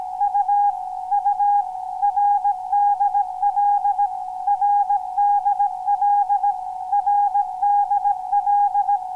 Morse code, heard as a keyed tone, about 17 wpm, 820 Hz. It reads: U U RDL RDL RDL